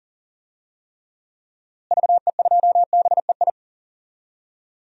Morse code, heard as a keyed tone, about 40 wpm, 700 Hz. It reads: VE2BEI
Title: DXCC-100 Call Signs - 6 Characters - Repeated 3x - 40wpm